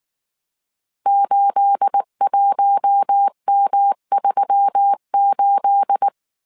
Morse code, heard as a keyed tone, about 19 wpm, 800 Hz. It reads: 81M38
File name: CW / DET / 1 9 1500C S A 18wpm 2459